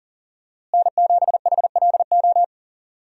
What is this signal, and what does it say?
Morse code, heard as a keyed tone, about 40 wpm, 700 Hz. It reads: N7HLO
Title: USA Call Signs - 5 Characters - Repeated 2x - 40wpm